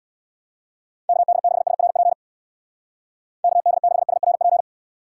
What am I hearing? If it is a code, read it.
Morse code, heard as a keyed tone, about 65 wpm, 700 Hz. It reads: KD6SRP KD6SRP